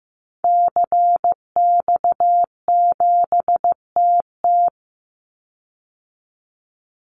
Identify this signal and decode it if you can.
Morse code, heard as a keyed tone, about 15 wpm, 700 Hz. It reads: CX7TT
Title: Call Signs Repeated 3x - 15wpm